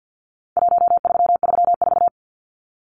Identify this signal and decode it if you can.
Morse code, heard as a keyed tone, about 50 wpm, 700 Hz. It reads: 1334
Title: Numbers - Contest Style - 4-Digit 50wpm